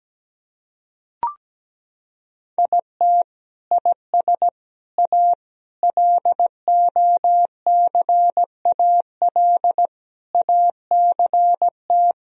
Morse code, heard as a keyed tone, about 17 wpm, 700 Hz. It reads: IT IS A LOCAL ACT
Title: Sentences from Top 1000 Words - Review Part 3 - 17wpm